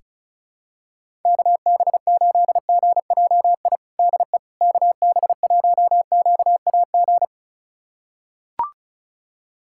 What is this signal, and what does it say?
Morse code, heard as a keyed tone, about 36 wpm, 700 Hz. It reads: KB8GJI DE KB1QAG
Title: USA Callsign DE Callsign 35wpm